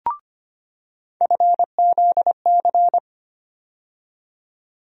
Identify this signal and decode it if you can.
Morse code, heard as a keyed tone, about 25 wpm, 700 Hz. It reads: FZC
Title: USA Call Sign Postfixes - 3 Letters 25wpm